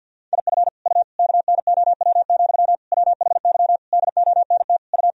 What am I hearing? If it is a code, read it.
Morse code, heard as a keyed tone, about 52 wpm, 700 Hz. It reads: IF U KNOW, WHY DONT U
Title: Sentences from Top 300 Words and Abbreviations 50wpm